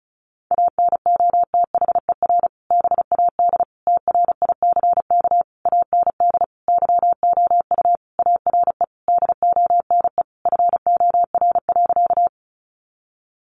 Morse code, heard as a keyed tone, about 34 wpm, 700 Hz. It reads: ANOTHER BAD TRICK AND YOU ARE DONE FOR.
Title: Sentences from Top 1500 Words 35wpm